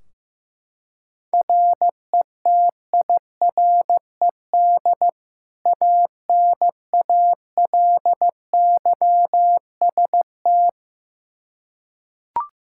Morse code, heard as a keyed tone, about 16 wpm, 700 Hz. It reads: RETIRED ANALYST
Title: Practice QSO - Exchange 3 - More Info - 15wpm